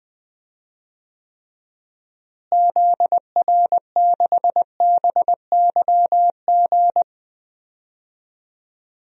Morse code, heard as a keyed tone, about 20 wpm, 700 Hz. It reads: ZR6BYG
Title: DXCC-100 Call Signs - 6 Characters - Repeated 2x - 20wpm